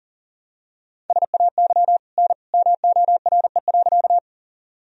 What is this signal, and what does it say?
Morse code, heard as a keyed tone, about 40 wpm, 700 Hz. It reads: SAY N MORE.
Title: Sentences from Top 200 Words and Abbreviations 40wpm